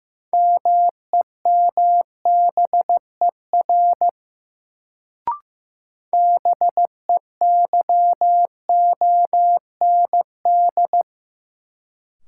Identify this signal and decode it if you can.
Morse code, heard as a keyed tone, about 15 wpm, 700 Hz. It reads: MEMBER BEYOND
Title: Sentences from Top 1100 Words - New Word Drill Alt - 15wpm